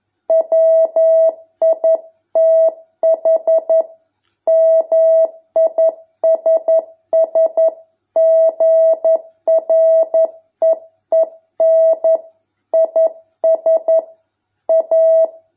Morse code, heard as a keyed tone, about 11 wpm, 650 Hz. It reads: WITH MISS GREEN IS A